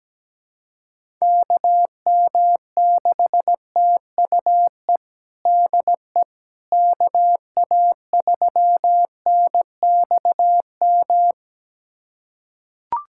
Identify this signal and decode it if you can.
Morse code, heard as a keyed tone, about 17 wpm, 700 Hz. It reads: KM6TUE DE KA3NXM
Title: USA Callsign DE Callsign 17wpm